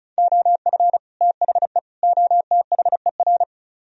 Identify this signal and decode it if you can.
Morse code, heard as a keyed tone, about 45 wpm, 700 Hz. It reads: OF THE OTHER